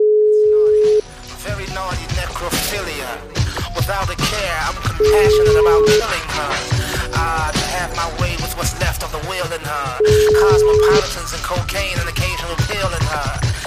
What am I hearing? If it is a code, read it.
Ringback tone (Europe, 425 Hz), with speech.